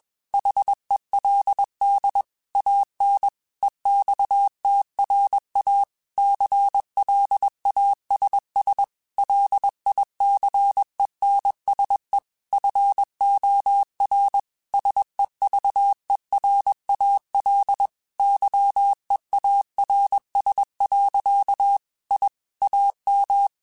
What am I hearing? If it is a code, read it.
Morse code, heard as a keyed tone, about 21 wpm, 800 Hz. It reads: HELD AN EXTRA CLASS LICENSE FOR SEVERAL YEARS. I AM
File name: QSOs / QSOs-mp3s / kc7haa@20